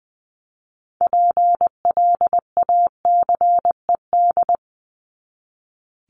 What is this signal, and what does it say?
Morse code, heard as a keyed tone, about 20 wpm, 700 Hz. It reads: PLACED
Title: ED Suffix - Top 100 Words with Base Words 20wpm